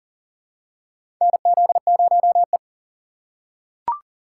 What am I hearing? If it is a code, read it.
Morse code, heard as a keyed tone, about 40 wpm, 700 Hz. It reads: NZ0E
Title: USA Call Signs - 4 Characters - 2x Character Spacing - 40wpm